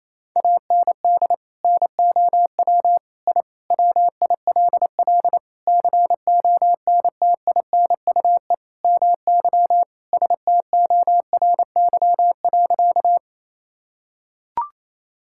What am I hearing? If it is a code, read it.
Morse code, heard as a keyed tone, about 27 wpm, 700 Hz. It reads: AND NOW I WILL CONTINUE MY STORY.
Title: Sentences from Top 1000 Words - Review Part 6 - 28wpm